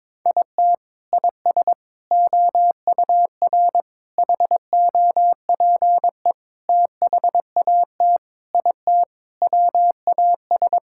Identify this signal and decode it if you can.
Morse code, heard as a keyed tone, about 22 wpm, 700 Hz. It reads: IT IS OUR HOPE THAT IT WAS
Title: Sentences from Top 1000 Words - Review Part 5 - 22wpm